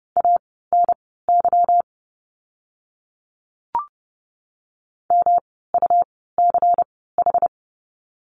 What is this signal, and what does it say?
Morse code, heard as a keyed tone, about 30 wpm, spaced as for 17 wpm, 700 Hz. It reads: ANY MUCH
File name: course / 30:10wpm / 230 Lesson 37f - Letter x - Words 30wpm